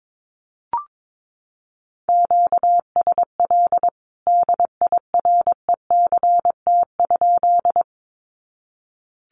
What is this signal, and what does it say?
Morse code, heard as a keyed tone, about 22 wpm, 700 Hz. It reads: QSL DIRECT?
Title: Common QSO Phrases 22wpm